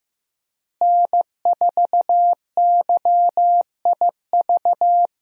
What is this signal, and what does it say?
Morse code, heard as a keyed tone, about 15 wpm, 700 Hz. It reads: N4YIV
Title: USA Call Signs - 5 Characters - Repeated 2x - 15wpm